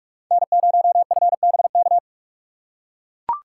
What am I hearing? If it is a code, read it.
Morse code, heard as a keyed tone, about 42 wpm, 700 Hz. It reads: N0FBK
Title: USA Call Signs - 5 Characters - 4x Character Spacing - 45wpm